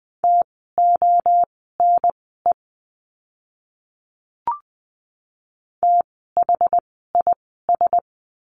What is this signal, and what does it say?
Morse code, heard as a keyed tone, about 20 wpm, spaced as for 14 wpm, 700 Hz. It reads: TONE THIS
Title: 068 Lesson 12f - Letter d - Words 20wpm